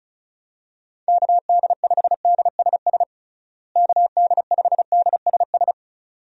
Morse code, heard as a keyed tone, about 34 wpm, 700 Hz. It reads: KD5DSS KD5DSS
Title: USA Call Signs - 6 Characters - Repeated 2x - 35wpm